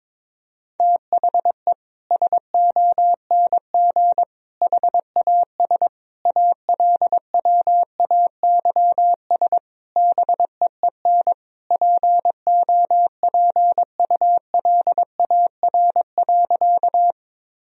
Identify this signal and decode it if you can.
Morse code, heard as a keyed tone, about 21 wpm, 700 Hz. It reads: THE SONG HAS ALWAYS BEEN POPULAR.
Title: Sentences from Top 1000 Words - Review Part 3 - 22wpm